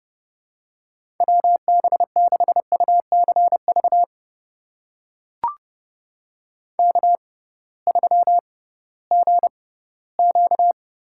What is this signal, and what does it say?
Morse code, heard as a keyed tone, about 30 wpm, spaced as for 17 wpm, 700 Hz. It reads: WB6UCV K3GQ
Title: USA Call Signs - 6x Character Spacing - 30wpm